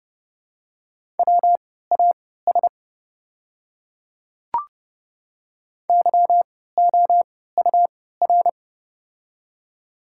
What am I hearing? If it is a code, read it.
Morse code, heard as a keyed tone, about 30 wpm, spaced as for 17 wpm, 700 Hz. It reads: WAS YOUR
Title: 3x Character Spacing - Top 300 Words - 30wpm